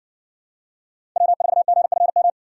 Morse code, heard as a keyed tone, about 60 wpm, 700 Hz. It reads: W4GFM